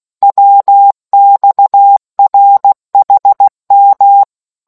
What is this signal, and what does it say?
Morse code, heard as a keyed tone, about 16 wpm, 800 Hz. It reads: WXRHM